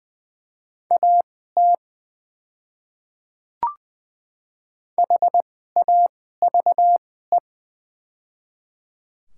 Morse code, heard as a keyed tone, about 20 wpm, spaced as for 14 wpm, 700 Hz. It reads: AT HAVE
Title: 2x Character Spacing - Top 100 Words - 20wpm